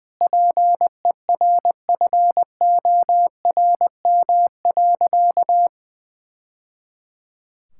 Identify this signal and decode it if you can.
Morse code, heard as a keyed tone, about 21 wpm, 700 Hz. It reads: PERFORM.
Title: Sentences from Top 1000 Words - Review Part 5 - 20wpm